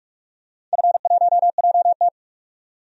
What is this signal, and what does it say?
Morse code, heard as a keyed tone, about 48 wpm, 700 Hz. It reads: F1JT
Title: DXCC-100 Call Signs - 4 Characters - 45wpm